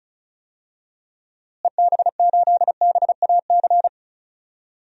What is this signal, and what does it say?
Morse code, heard as a keyed tone, about 34 wpm, 700 Hz. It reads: EB8BAC
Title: DXCC-100 Call Signs - 6 Characters - Repeated 2x - 35wpm